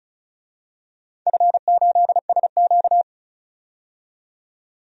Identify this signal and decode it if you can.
Morse code, heard as a keyed tone, about 34 wpm, 700 Hz. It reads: F8SQ
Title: DXCC-100 Call Signs - 4 Characters - 35wpm